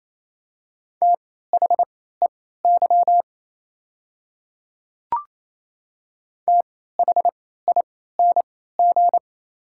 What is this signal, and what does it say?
Morse code, heard as a keyed tone, about 28 wpm, spaced as for 16 wpm, 700 Hz. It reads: THEY THING
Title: 3x Character Spacing - Top 100 Words - 28wpm